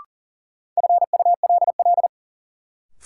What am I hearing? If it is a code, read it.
Morse code, heard as a keyed tone, about 40 wpm, 700 Hz. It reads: FULL
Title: Sentences from Top 500 Words - New Word Drill - 40wpm